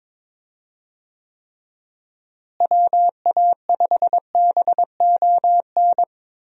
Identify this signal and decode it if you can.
Morse code, heard as a keyed tone, about 21 wpm, 700 Hz. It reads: WA5BON
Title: USA Call Signs - 6 Characters - Repeated 2x - 22wpm